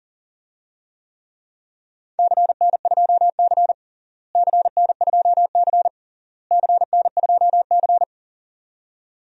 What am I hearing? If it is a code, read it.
Morse code, heard as a keyed tone, about 40 wpm, 700 Hz. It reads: CN2C CN2C CN2C
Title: DXCC-100 Call Signs - 4 Characters - Repeated 3x - 40wpm